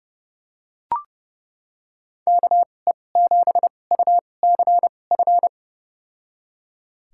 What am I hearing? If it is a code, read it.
Morse code, heard as a keyed tone, about 30 wpm, spaced as for 22 wpm, 700 Hz. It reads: KE7UCF
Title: USA Call Signs - 6 Characters - 2x Character Spacing - 30wpm